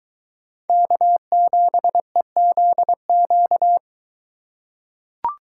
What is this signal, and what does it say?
Morse code, heard as a keyed tone, about 23 wpm, 700 Hz. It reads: K7EZQ